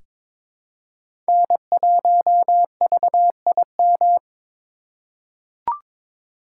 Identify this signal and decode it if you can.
Morse code, heard as a keyed tone, about 22 wpm, 700 Hz. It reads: N1VIM
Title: USA Call Signs - 5 Characters - 2x Character Spacing - 22wpm